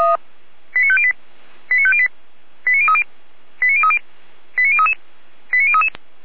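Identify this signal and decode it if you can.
Five-tone selective call (ZVEI), heard as quick runs of five tones, tones of about 70 ms. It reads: 89508 89508 89030 89030 89033 89033
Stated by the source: ZVEI five-tone call 89042 received by SDR.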